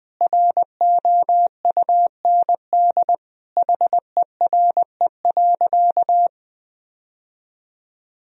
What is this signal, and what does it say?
Morse code, heard as a keyed tone, about 21 wpm, 700 Hz. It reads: ROUND HERE.
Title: Sentences from Top 1000 Words - Review Part 1 - 20wpm